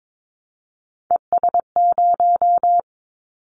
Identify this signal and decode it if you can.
Morse code, heard as a keyed tone, about 22 wpm, 700 Hz. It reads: ES0
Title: Callsign Prefixes for Top 100 DXCC - 22wpm